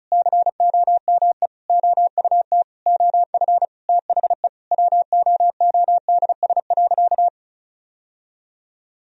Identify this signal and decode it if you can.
Morse code, heard as a keyed tone, about 36 wpm, 700 Hz. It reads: COME OUT OF THE WOODS.
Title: Sentences from Top 1000 Words - Review Part 2 - 35wpm